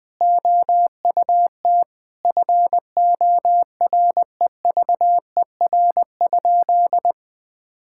Morse code, heard as a keyed tone, about 19 wpm, 700 Hz. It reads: OUT FOREVER?
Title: Sentences from Top 1600 Words 20wpm